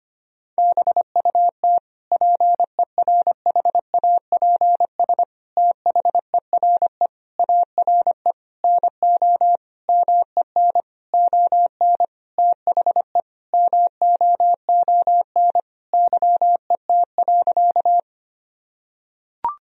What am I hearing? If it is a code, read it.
Morse code, heard as a keyed tone, about 25 wpm, 700 Hz. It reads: BUT PERHAPS THERE ARE NO MEN ON THE MOON YET.
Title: Sentences from Top 1000 Words - Review Part 3 - 25wpm